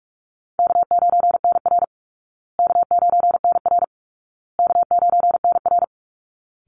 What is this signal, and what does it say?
Morse code, heard as a keyed tone, about 44 wpm, 700 Hz. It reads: K9NR K9NR K9NR
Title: Easy Call Signs Repeated 3x - 45wpm